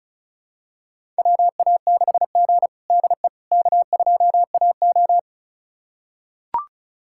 Morse code, heard as a keyed tone, about 36 wpm, 700 Hz. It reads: WA6G DE K2AO
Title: USA Callsign DE Callsign - 4 Characters - 35wpm